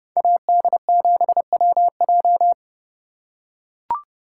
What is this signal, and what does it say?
Morse code, heard as a keyed tone, about 29 wpm, 700 Hz. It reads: AD7WJ